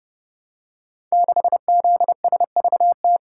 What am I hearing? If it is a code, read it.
Morse code, heard as a keyed tone, about 30 wpm, 700 Hz. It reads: 6ZSVT
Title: Code Groups - 5 Letter:Numbers - 30wpm